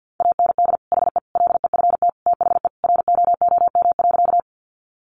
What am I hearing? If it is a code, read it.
Morse code, heard as a keyed tone, about 50 wpm, 700 Hz. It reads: AND HE LEFT THE ROOM.
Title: Sentences from Top 300 Words 50wpm